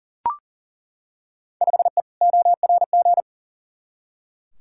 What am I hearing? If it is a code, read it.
Morse code, heard as a keyed tone, about 40 wpm, 700 Hz. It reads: 5E ORG
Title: Field Day Exchanges Encore - 40wpm